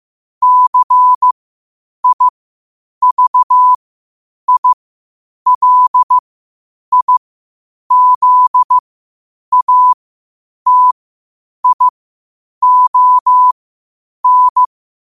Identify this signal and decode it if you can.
Morse code, heard as a keyed tone, about 15 wpm, spaced as for 8 wpm, 1000 Hz. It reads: CIVILIZATION
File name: qotdfiles / 8wpm